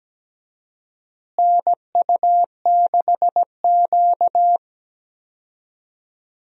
Morse code, heard as a keyed tone, about 17 wpm, 700 Hz. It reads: NU6Q
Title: USA Call Signs - 4 Characters - Repeated 3x - 17wpm